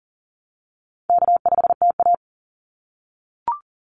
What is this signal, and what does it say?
Morse code, heard as a keyed tone, about 40 wpm, 700 Hz. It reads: K5TA
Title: Easy Call Signs - 40wpm